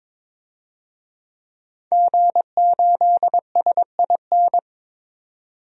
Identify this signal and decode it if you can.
Morse code, heard as a keyed tone, about 22 wpm, 700 Hz. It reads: G8SIN
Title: DXCC-100 Call Signs - 5 Characters - 22wpm